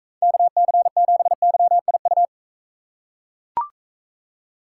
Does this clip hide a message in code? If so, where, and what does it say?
Yes: Morse code from 0.2 to 2.3 s, heard as a keyed tone, about 41 wpm, 700 Hz. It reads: KC7YIU